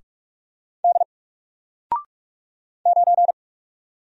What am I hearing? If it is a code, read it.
Morse code, heard as a keyed tone, about 46 wpm, 700 Hz. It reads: D 9
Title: Single Character (Letter:Number:Punctuation:Prosign) 45wpm